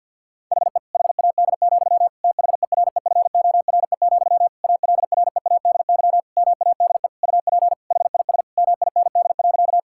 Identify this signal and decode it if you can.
Morse code, heard as a keyed tone, about 50 wpm, 700 Hz. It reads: HE HAD, THEREFORE, ALREADY MADE UP HIS MIND.
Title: Sentences from Top 1000 Words - Review Part 5 - 50wpm